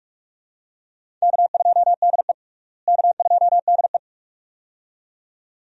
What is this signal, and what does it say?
Morse code, heard as a keyed tone, about 42 wpm, 700 Hz. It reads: K2DE K2DE
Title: USA Call Signs - 4 Characters - Repeated 2x - 45wpm